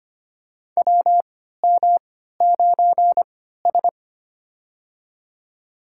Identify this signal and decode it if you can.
Morse code, heard as a keyed tone, about 25 wpm, spaced as for 14 wpm, 700 Hz. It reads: WM9S
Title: USA Call Signs - 4 Characters - 3x Character Spacing - 25wpm